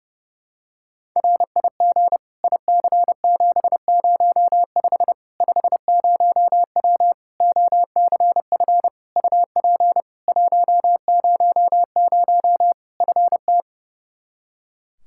Morse code, heard as a keyed tone, about 32 wpm, 700 Hz. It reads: RIG IC705 50W OCF UP 100 FT
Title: Practice QSO - Exchange 2 - Info - 30wpm